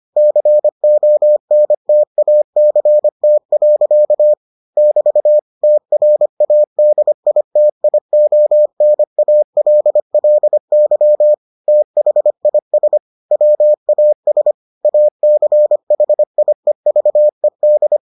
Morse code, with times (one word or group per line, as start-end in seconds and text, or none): 0.2-4.3 CONTACT.
4.8-5.4 <BT>
5.6-11.3 TRADITIONALLY
11.7-13.0 THIS
13.3-14.5 WAS
14.8-18.0 ACHIEVED